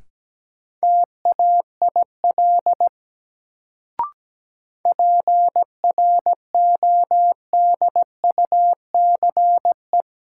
Morse code, heard as a keyed tone, about 17 wpm, 700 Hz. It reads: TAIL PRODUCE
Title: Sentences from Top 1000 Words - Review Part 3 - New Word Drill - 17wpm